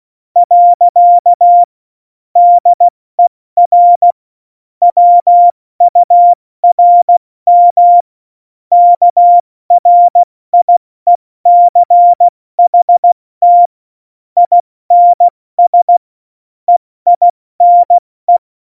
Morse code, heard as a keyed tone, about 16 wpm, 700 Hz. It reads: . DER WURM KRIECHT INS EINE
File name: Morsekurs / Witze / Witz 13 March 2018, 70 F80